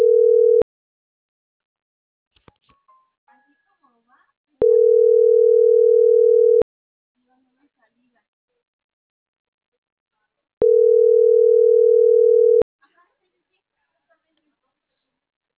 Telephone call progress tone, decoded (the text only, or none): ringback tone